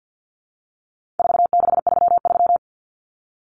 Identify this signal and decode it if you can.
Morse code, heard as a keyed tone, about 50 wpm, 700 Hz. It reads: 4633
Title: Common Cut Numbers - 4-Digit 50wpm